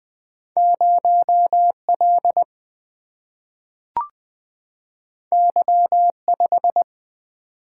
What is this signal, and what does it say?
Morse code, heard as a keyed tone, about 20 wpm, 700 Hz. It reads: 0L Y5
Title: Single-Letter Single-Number 20wpm